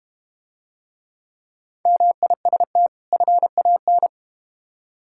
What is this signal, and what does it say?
Morse code, heard as a keyed tone, about 32 wpm, 700 Hz. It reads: MIST FAN